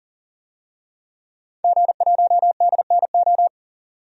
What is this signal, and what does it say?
Morse code, heard as a keyed tone, about 40 wpm, 700 Hz. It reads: G1DNO
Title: DXCC-100 Call Signs - 5 Characters - 40wpm